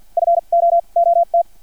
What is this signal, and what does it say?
Morse code, heard as a keyed tone, about 50 wpm, spaced as for 38 wpm, 680 Hz. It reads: FOOT
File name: archived / CRSnet